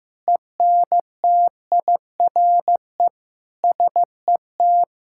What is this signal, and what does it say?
Morse code, heard as a keyed tone, about 15 wpm, 700 Hz. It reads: ENTIRE SET
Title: Sentences from Top 1000 Words - Review Part 5 - 15wpm